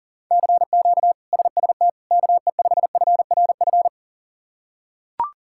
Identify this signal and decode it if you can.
Morse code, heard as a keyed tone, about 40 wpm, 700 Hz. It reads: CQ SST KE5FRF
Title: K1USN SST QSO Practice - 40wpm